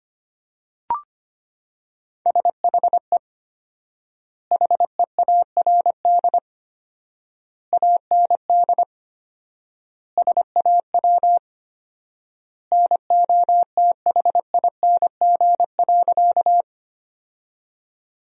Morse code, heard as a keyed tone, about 25 wpm, 700 Hz. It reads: SHE HEARD AND SAW NOTHING.
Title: 4x Word-Spacing - Sentences from Top 500 Words - 25wpm